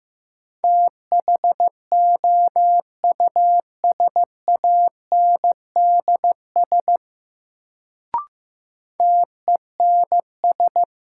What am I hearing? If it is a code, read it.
Morse code, heard as a keyed tone, about 15 wpm, 700 Hz. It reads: THOUSANDS TENS
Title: Sets of 4 Words - New Word Drill - 15wpm